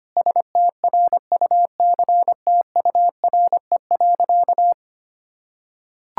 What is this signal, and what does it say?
Morse code, heard as a keyed tone, about 25 wpm, 700 Hz. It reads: STRUCTURE.
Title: Sentences from Top 1000 Words - Review Part 6 - 25wpm